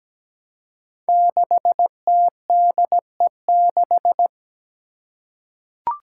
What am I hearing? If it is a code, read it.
Morse code, heard as a keyed tone, about 17 wpm, 700 Hz. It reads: 6TDE6
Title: Five Challenging Letter-Number Combinations 17wpm